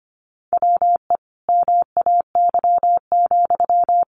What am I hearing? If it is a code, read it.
Morse code, heard as a keyed tone, about 25 wpm, 700 Hz. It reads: WE MAY,
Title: Sentences from Top 1500 Words 25wpm